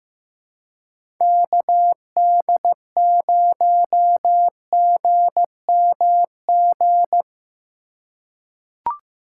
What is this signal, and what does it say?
Morse code, heard as a keyed tone, about 15 wpm, 700 Hz. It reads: KD0GMG
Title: USA Call Signs - 6 Characters - 2x Character Spacing - 15wpm